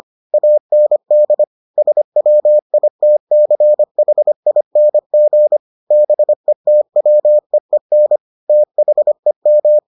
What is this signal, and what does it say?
Morse code, heard as a keyed tone, about 25 wpm, 600 Hz. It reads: AND SWITCHING BETWEEN THEM